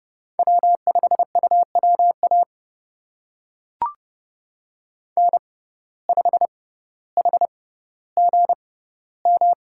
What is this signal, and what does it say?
Morse code, heard as a keyed tone, about 32 wpm, spaced as for 16 wpm, 700 Hz. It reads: W5UWA N5HGM
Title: USA Call Signs - 5 Characters - 6x Character Spacing - 30wpm